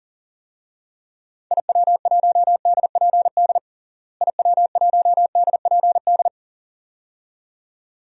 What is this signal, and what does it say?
Morse code, heard as a keyed tone, about 37 wpm, 700 Hz. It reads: IW1DPD IW1DPD
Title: DXCC-100 Call Signs - 6 Characters - Repeated 2x - 40wpm